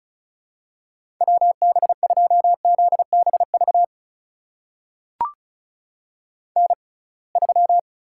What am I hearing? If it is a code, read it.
Morse code, heard as a keyed tone, about 36 wpm, spaced as for 27 wpm, 700 Hz. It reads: WB2ZBV N3
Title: USA Call Signs - 6x Character Spacing - 35wpm